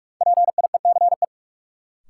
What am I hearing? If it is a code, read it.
Morse code, heard as a keyed tone, about 45 wpm, 700 Hz. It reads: PIECE
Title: Sets of 3 Words - New Word Drill - 45wpm